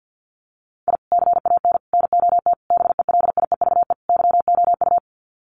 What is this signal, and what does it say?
Morse code, heard as a keyed tone, about 50 wpm, 700 Hz. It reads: I CAN NOT BELIEVE YOU
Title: Sets of 5 Words Encore 50wpm